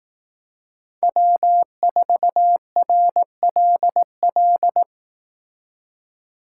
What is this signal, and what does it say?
Morse code, heard as a keyed tone, about 19 wpm, 700 Hz. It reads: W4RLL
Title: Speed Racing - USA 1x3 Call Signs 15wpm